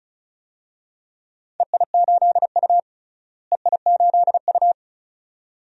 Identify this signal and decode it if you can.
Morse code, heard as a keyed tone, about 34 wpm, 700 Hz. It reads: EI8U EI8U
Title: DXCC-100 Call Signs - 4 Characters - Repeated 2x - 35wpm